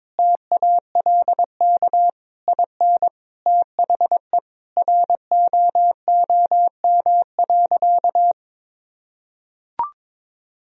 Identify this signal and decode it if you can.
Morse code, heard as a keyed tone, about 22 wpm, 700 Hz. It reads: TALK IN THE ROOM.
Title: Sentences from Top 1000 Words - Review Part 4 - 22wpm